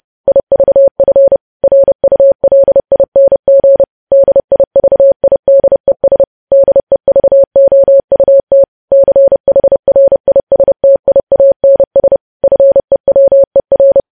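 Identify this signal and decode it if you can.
Morse code, heard as a keyed tone, about 30 wpm, 550 Hz. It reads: IVF RULING DIVIDES DEVOUT CHRISTIANS FEWER
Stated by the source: News headlines sent in Morse at 30 words per minute.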